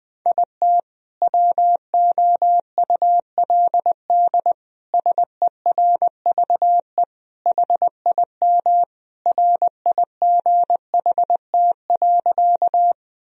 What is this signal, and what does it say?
Morse code, heard as a keyed tone, about 20 wpm, 700 Hz. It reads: IT WOULD SERVE HIM RIGHT.
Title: Sentences from Top 1000 Words - Review Part 2 - 20wpm